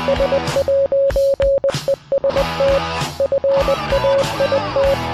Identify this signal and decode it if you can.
Morse code, heard as a keyed tone, about 20 wpm, 550 Hz. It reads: S'EST FAIT